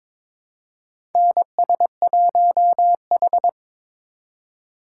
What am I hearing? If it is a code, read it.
Morse code, heard as a keyed tone, about 22 wpm, 700 Hz. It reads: NS1H
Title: USA Call Signs - 3x Character Spacing - 22wpm